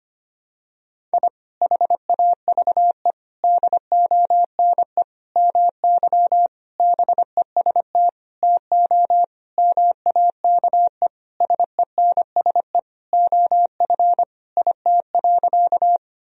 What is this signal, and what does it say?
Morse code, heard as a keyed tone, about 25 wpm, 700 Hz. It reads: I HAVE DONE MY BEST TO MAKE SENSE OF IT.
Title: Sentences from Top 1000 Words - Review Part 4 - 25wpm